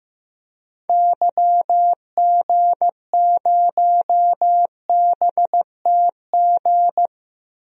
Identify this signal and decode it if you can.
Morse code, heard as a keyed tone, about 15 wpm, 700 Hz. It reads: YG0BTG
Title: DXCC-100 Call Signs - 6 Characters - 15wpm